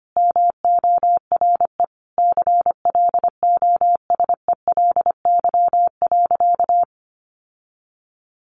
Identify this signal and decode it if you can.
Morse code, heard as a keyed tone, about 25 wpm, 700 Hz. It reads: MORE CLOSELY.
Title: Sentences from Top 1400 Words 25wpm